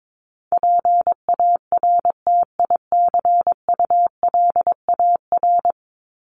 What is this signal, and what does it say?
Morse code, heard as a keyed tone, about 22 wpm, 700 Hz. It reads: PARTICULAR
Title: LY Suffix - Top 100 Words with Base Words 22wpm